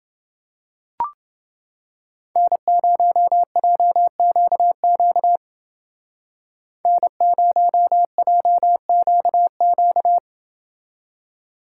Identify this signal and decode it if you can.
Morse code, heard as a keyed tone, about 30 wpm, 700 Hz. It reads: N0JQQ N0JQQ